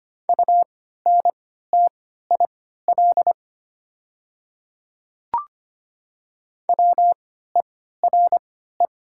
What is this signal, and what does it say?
Morse code, heard as a keyed tone, about 25 wpm, spaced as for 14 wpm, 700 Hz. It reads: UNTIL WERE
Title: 3x Character Spacing - Top 300 Words - 25wpm